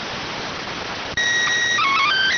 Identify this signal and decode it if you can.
SSTV (slow-scan television), PD 290.